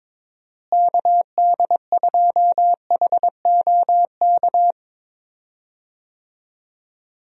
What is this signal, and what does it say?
Morse code, heard as a keyed tone, about 22 wpm, 700 Hz. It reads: KD2HOK
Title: USA Call Signs - 6 Characters - Repeated 2x - 22wpm